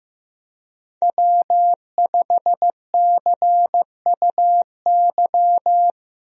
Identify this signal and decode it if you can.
Morse code, heard as a keyed tone, about 16 wpm, 700 Hz. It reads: W5CUY
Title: USA Call Signs - 5 Characters - Repeated 2x - 15wpm